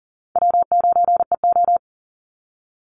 Morse code, heard as a keyed tone, about 40 wpm, 700 Hz. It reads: W9EO
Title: Easy Call Signs - 40wpm